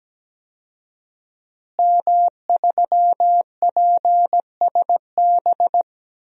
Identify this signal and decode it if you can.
Morse code, heard as a keyed tone, about 17 wpm, 700 Hz. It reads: M3PSB
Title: DXCC-100 Call Signs - 5 Characters - Repeated 3x - 17wpm